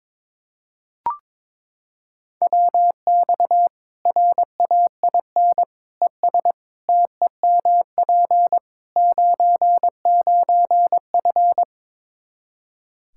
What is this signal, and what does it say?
Morse code, heard as a keyed tone, about 22 wpm, 700 Hz. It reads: WX RAIN ES TEMP 99F
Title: Practice QSO - Exchange 2 - Info - 22wpm